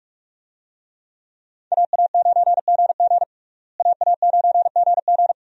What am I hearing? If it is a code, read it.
Morse code, heard as a keyed tone, about 42 wpm, 700 Hz. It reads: AA9GG AA9GG
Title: USA Call Signs - 5 Characters - Repeated 2x - 45wpm